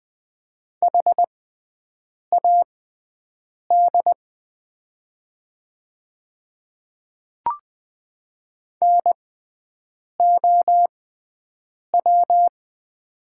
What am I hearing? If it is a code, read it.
Morse code, heard as a keyed tone, about 20 wpm, spaced as for 7 wpm, 700 Hz. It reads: HAD NOW
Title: 6x Character Spacing - Top 100 Words - 20wpm